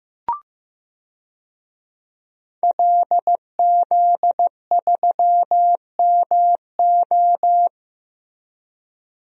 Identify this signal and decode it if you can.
Morse code, heard as a keyed tone, about 15 wpm, 700 Hz. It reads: LZ3MO
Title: DXCC-100 Call Signs - 5 Characters - Repeated 2x - 15wpm